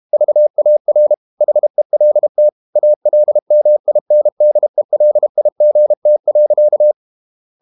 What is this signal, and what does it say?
Morse code, heard as a keyed tone, about 32 wpm, 600 Hz. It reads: VAR HELT ALMINDELIGT.
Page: Ferie 32 wpm